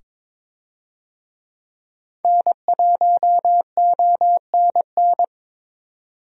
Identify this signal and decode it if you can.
Morse code, heard as a keyed tone, about 22 wpm, 700 Hz. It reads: N1ONN
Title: USA Call Signs - 5 Characters - 22wpm